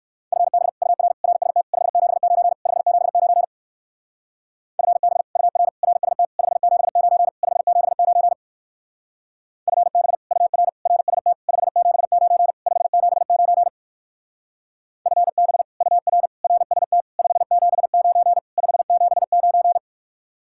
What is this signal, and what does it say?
Morse code, heard as a keyed tone, about 56 wpm, 700 Hz. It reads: FB UR RST 579 579 FB UR RST 579 579 FB UR RST 579 579 FB UR RST 579 579
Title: Speed-Racing Common QSO Phrases - 45wpm